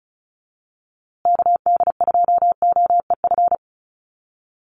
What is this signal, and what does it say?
Morse code, heard as a keyed tone, about 34 wpm, 700 Hz. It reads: KD2OEF
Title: USA Call Signs Repeated 3x 35wpm